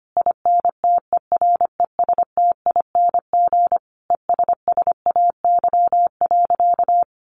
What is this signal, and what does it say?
Morse code, heard as a keyed tone, about 25 wpm, 700 Hz. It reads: INTERESTING ESSAY.
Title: Sentences from Top 1500 Words 25wpm